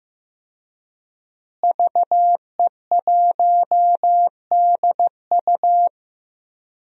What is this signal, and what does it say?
Morse code, heard as a keyed tone, about 15 wpm, 700 Hz. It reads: VE1DU
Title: DXCC-100 Call Signs - 5 Characters - Repeated 2x - 15wpm